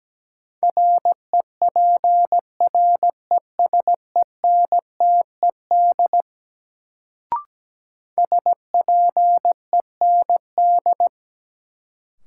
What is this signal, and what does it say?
Morse code, heard as a keyed tone, about 17 wpm, 700 Hz. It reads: REPRESENTED SPEND
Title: Sets of 5 Words - New Word Drill - 17wpm